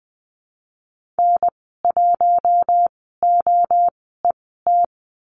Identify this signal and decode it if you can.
Morse code, heard as a keyed tone, about 20 wpm, spaced as for 14 wpm, 700 Hz. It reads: N1OET
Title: 076 Lesson 13g - Letter L - Call Signs 20wpm